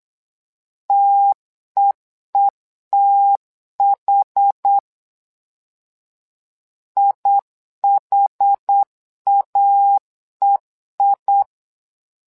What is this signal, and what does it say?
Morse code, heard as a keyed tone, about 8 wpm, 800 Hz. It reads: TEETH IHAEI
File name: LEZIONI / 2 UHDN / 2 UH 8wpm 2129